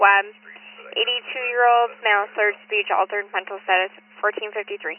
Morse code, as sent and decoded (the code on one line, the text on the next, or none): none